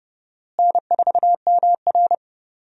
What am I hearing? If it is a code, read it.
Morse code, heard as a keyed tone, about 32 wpm, 700 Hz. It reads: N4MR
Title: USA Call Signs - 4 Characters - Repeated 3x - 30wpm